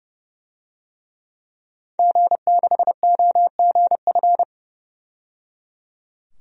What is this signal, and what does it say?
Morse code, heard as a keyed tone, about 30 wpm, 700 Hz. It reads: G6OGF
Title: DXCC-100 Call Signs - 5 Characters - 30wpm